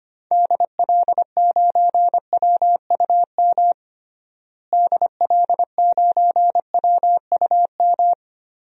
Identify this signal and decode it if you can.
Morse code, heard as a keyed tone, about 25 wpm, 700 Hz. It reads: DL9WUM DL9WUM
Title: DXCC-100 Call Signs - 6 Characters - Repeated 2x - 25wpm